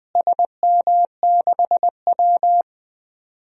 Morse code, heard as a keyed tone, about 20 wpm, 700 Hz. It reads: SM6W
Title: DXCC-100 Call Signs - 4 Characters - Repeated 3x - 20wpm